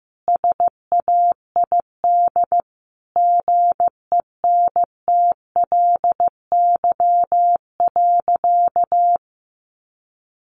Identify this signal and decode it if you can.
Morse code, heard as a keyed tone, about 15 wpm, 700 Hz. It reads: SAID GENTLY.
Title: Sentences from Top 1500 Words 15wpm